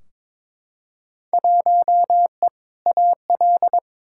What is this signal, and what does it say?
Morse code, heard as a keyed tone, about 22 wpm, 700 Hz. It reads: 1E AL
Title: Field Day Exchanges - 22wpm